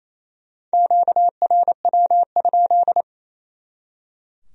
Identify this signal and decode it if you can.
Morse code, heard as a keyed tone, about 34 wpm, 700 Hz. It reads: QRW?
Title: Speed-Racing Q-Codes - 28wpm